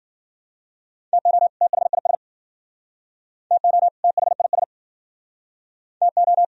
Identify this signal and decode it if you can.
Morse code, heard as a keyed tone, about 54 wpm, 700 Hz. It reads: TO THIS TO THIS TO